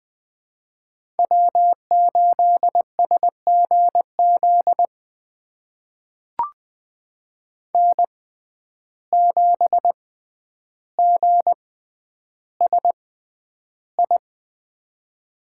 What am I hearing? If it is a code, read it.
Morse code, heard as a keyed tone, about 20 wpm, spaced as for 10 wpm, 700 Hz. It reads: W8SGZ N7GSI
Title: USA Call Signs - 5 Characters - 6x Character Spacing - 20wpm